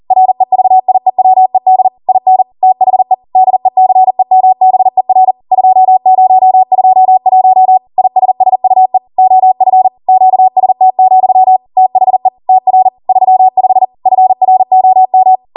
Morse code, heard as a keyed tone, about 40 wpm, 750 Hz. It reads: REVIEWED IN THE DECEMBER 2021 ISSUE OF QST, THE TR 35 FROM
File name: Morse / 40WPM